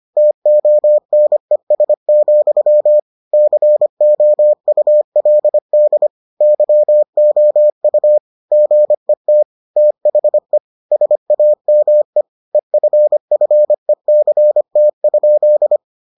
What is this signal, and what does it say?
Morse code, heard as a keyed tone, about 25 wpm, 600 Hz. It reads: TONES, COULD YOU GET THE SAME EFFECT?